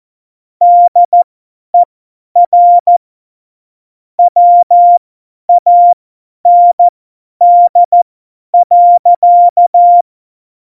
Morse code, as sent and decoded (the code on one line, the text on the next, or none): -.. .-. / .-- .- -. -.. .-.-.-
DER WAND.